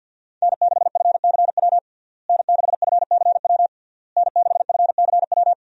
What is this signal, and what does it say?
Morse code, heard as a keyed tone, about 52 wpm, 700 Hz. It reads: N6FCW N6FCW N6FCW
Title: USA Call Signs - 5 Characters - Repeated 3x - 50wpm